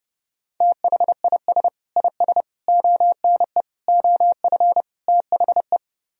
Morse code, heard as a keyed tone, about 30 wpm, 700 Hz. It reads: THIS IS ONE OF THE